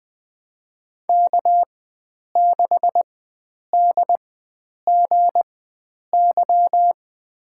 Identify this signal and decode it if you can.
Morse code, heard as a keyed tone, about 20 wpm, spaced as for 9 wpm, 700 Hz. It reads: K6DGY
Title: USA Call Signs - 5 Characters - 4x Character Spacing - 20wpm